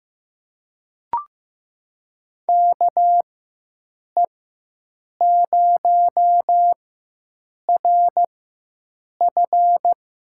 Morse code, heard as a keyed tone, about 15 wpm, spaced as for 7 wpm, 700 Hz. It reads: KE0RF
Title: USA Call Signs - 4x Character Spacing - 15wpm